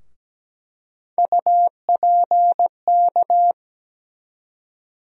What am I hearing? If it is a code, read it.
Morse code, heard as a keyed tone, about 17 wpm, 700 Hz. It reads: UPK
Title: USA Call Sign Postfixes - 3 Letters 17wpm